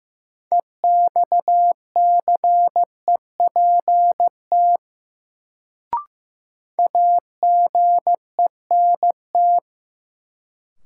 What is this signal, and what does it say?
Morse code, heard as a keyed tone, about 15 wpm, 700 Hz. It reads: EXCEPT AGENT
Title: Sentences from Top 1600 Words - New Word Drill Alt - 15wpm